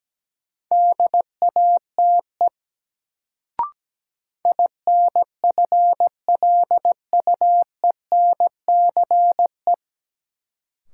Morse code, heard as a keyed tone, about 17 wpm, 700 Hz. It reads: DATE INFLUENCE
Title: Sentences from Top 1100 Words - New Word Drill Alt - 17wpm